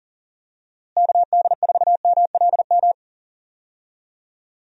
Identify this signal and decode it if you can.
Morse code, heard as a keyed tone, about 40 wpm, 700 Hz. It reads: KD4MLM
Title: USA Call Signs - 6 Characters - Repeated 2x - 40wpm